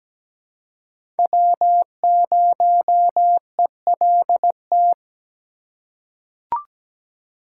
Morse code, heard as a keyed tone, about 17 wpm, 700 Hz. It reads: W0ELT
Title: USA Call Signs - 5 Characters - 4x Character Spacing - 17wpm